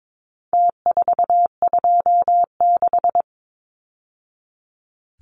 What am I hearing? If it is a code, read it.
Morse code, heard as a keyed tone, about 22 wpm, 700 Hz. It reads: T426
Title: Numbers - Contest Style - 4-Digit 22wpm